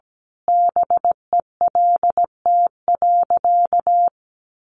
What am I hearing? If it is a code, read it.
Morse code, heard as a keyed tone, about 17 wpm, 700 Hz. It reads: BELT.
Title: Sentences from Top 1400 Words 17wpm